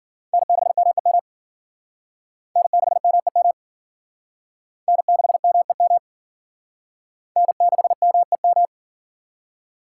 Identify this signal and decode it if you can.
Morse code, heard as a keyed tone, about 48 wpm, 700 Hz. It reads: N6MEM N6MEM N6MEM N6MEM
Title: Speed Racing - USA 1x3 Call Signs 40wpm